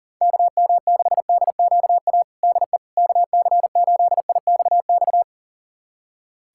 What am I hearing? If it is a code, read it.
Morse code, heard as a keyed tone, about 40 wpm, 700 Hz. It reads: KM6DQA DE KC8IXX
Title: USA Callsign DE Callsign 40wpm